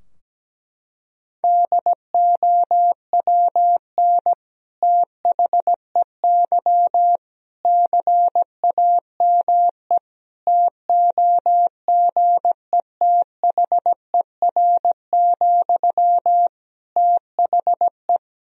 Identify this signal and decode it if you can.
Morse code, heard as a keyed tone, about 17 wpm, 700 Hz. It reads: DOWN THEY CAME TOGETHER, THE
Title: Sentences from Top 1000 Words - Review Part 2 - 17wpm